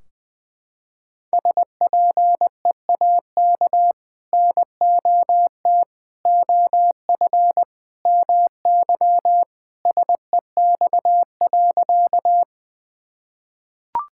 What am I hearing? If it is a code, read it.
Morse code, heard as a keyed tone, about 20 wpm, 700 Hz. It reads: SPEAK NOT OF MY SEX.
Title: Sentences from Top 1600 Words 20wpm